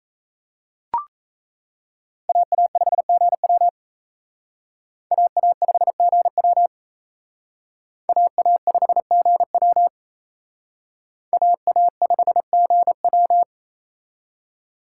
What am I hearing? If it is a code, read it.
Morse code, heard as a keyed tone, about 36 wpm, 700 Hz. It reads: AA5GW AA5GW AA5GW AA5GW